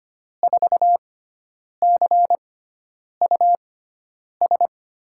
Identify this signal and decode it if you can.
Morse code, heard as a keyed tone, about 25 wpm, spaced as for 9 wpm, 700 Hz. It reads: 4CUS
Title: USA Call Signs - 6 Characters - 6x Character Spacing - 25wpm